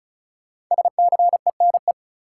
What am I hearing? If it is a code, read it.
Morse code, heard as a keyed tone, about 36 wpm, 700 Hz. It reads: SCENE
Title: Sentences from Top 1100 Words - New Word Drill - 35wpm